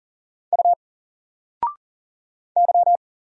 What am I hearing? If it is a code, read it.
Morse code, heard as a keyed tone, about 38 wpm, 700 Hz. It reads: U Y
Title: Single Character (Letter:Number:Punctuation:Prosign) 40wpm